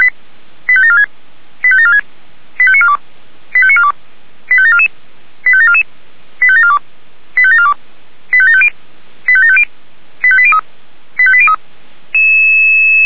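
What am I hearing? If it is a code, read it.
Five-tone selective call (ZVEI), heard as quick runs of five tones, tones of about 70 ms. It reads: 85746 85746 85942 85942 85744 85744 85742 85742 85750 85750 85803 85803
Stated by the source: ZVEI five-tone call 85749 received by SDR, on Feuerwehr Orlamünde audio server.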